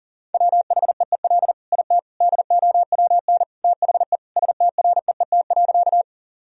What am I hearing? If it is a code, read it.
Morse code, heard as a keyed tone, about 40 wpm, 700 Hz. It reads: WHEEL IT DOWN THE STREET.
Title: Sentences from Top 1000 Words - Review Part 3 - 40wpm